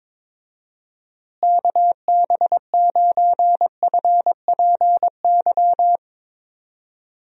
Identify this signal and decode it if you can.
Morse code, heard as a keyed tone, about 22 wpm, 700 Hz. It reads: KB9FPY
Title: USA Call Signs - 6 Characters - Repeated 3x - 22wpm